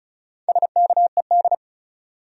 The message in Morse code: SKED

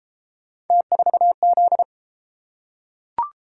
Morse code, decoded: T4Z

